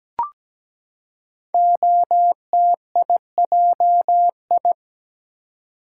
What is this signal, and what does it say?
Morse code, heard as a keyed tone, about 17 wpm, 700 Hz. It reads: OTIJI
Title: Code Groups - 5 Letters - 17wpm